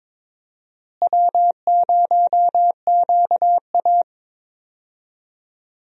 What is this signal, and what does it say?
Morse code, heard as a keyed tone, about 22 wpm, 700 Hz. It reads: W0QA
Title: USA Call Signs - 4 Characters - Repeated 3x - 22wpm